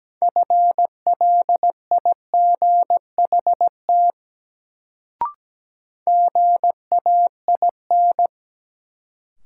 Morse code, heard as a keyed tone, about 17 wpm, 700 Hz. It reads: FLIGHT GAIN
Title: Sentences from Top 1000 Words - New Word Drill - 17wpm